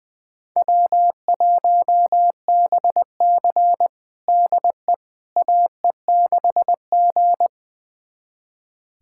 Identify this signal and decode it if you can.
Morse code, heard as a keyed tone, about 20 wpm, 700 Hz. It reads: W1BC DE AE6G
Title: USA Callsign DE Callsign - 4 Characters - 20wpm